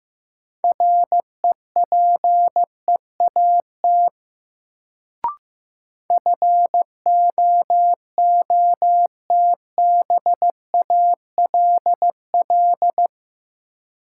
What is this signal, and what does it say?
Morse code, heard as a keyed tone, about 15 wpm, 700 Hz. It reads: REPEAT FOOTBALL
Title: Sentences from Top 1600 Words - New Word Drill - 15wpm